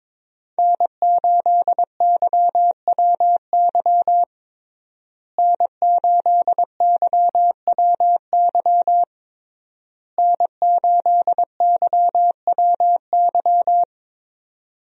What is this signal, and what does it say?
Morse code, heard as a keyed tone, about 22 wpm, 700 Hz. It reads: N8YWY N8YWY N8YWY
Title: USA Call Signs - 5 Characters - Repeated 3x - 22wpm